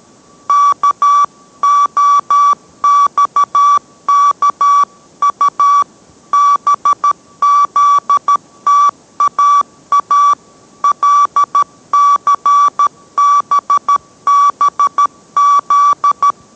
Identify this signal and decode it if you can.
Morse code, heard as a keyed tone, about 13 wpm, 1150 Hz. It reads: KOXKU BZTAA LCBBZ